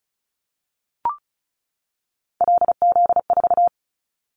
Morse code, heard as a keyed tone, about 34 wpm, 700 Hz. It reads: LZ4
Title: Callsign Prefixes for Top 100 DXCC - 35wpm